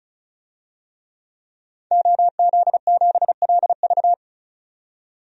Morse code, heard as a keyed tone, about 34 wpm, 700 Hz. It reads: OZ7LV
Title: DXCC-100 Call Signs - 5 Characters - 35wpm